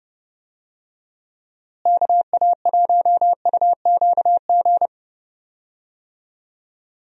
Morse code, heard as a keyed tone, about 30 wpm, 700 Hz. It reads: KA1UQG
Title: USA Call Signs - 6 Characters - Repeated 3x - 30wpm